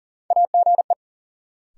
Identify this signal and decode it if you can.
Morse code, heard as a keyed tone, about 40 wpm, 700 Hz. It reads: AGE
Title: Sets of 3 Words - New Word Drill - 40wpm